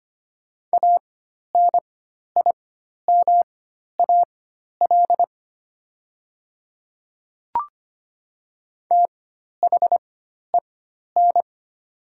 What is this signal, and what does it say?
Morse code, heard as a keyed tone, about 25 wpm, spaced as for 12 wpm, 700 Hz. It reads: ANIMAL THEN
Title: 4x Character Spacing - Top 200 Words - 25wpm